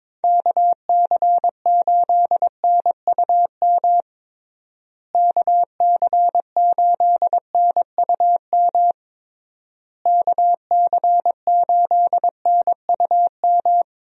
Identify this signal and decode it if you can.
Morse code, heard as a keyed tone, about 22 wpm, 700 Hz. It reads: KC8NUM KC8NUM KC8NUM
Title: USA Call Signs - 6 Characters - Repeated 3x - 22wpm